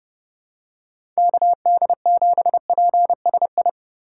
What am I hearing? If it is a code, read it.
Morse code, heard as a keyed tone, about 31 wpm, 700 Hz. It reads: KD7PSI
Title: USA Call Signs - 6 Characters - 4x Character Spacing - 30wpm